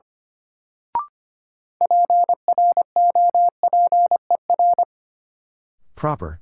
Morse code, heard as a keyed tone, about 25 wpm, 700 Hz. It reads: PROPER
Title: Sentences from Top 800 Words - New Word Drill - 25wpm